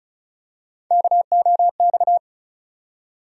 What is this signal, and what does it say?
Morse code, heard as a keyed tone, about 36 wpm, 700 Hz. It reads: KOX
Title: USA Call Sign Postfixes - 3 Letters 35wpm